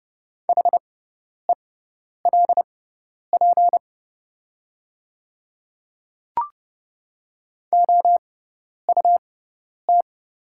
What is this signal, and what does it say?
Morse code, heard as a keyed tone, about 30 wpm, spaced as for 10 wpm, 690 Hz. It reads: HELP OUT
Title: 6x Character Spacing - Top 200 Words - 30wpm